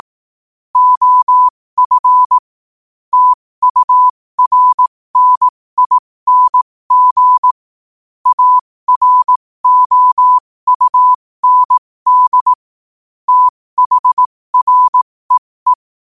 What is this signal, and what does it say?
Morse code, heard as a keyed tone, about 18 wpm, spaced as for 15 wpm, 1000 Hz. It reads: OF TURNING AROUND THREE
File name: qotdfiles / 15wpm